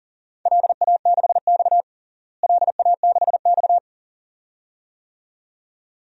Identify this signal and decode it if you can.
Morse code, heard as a keyed tone, about 37 wpm, 700 Hz. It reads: LA6X LA6X
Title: DXCC-100 Call Signs - 4 Characters - Repeated 2x - 40wpm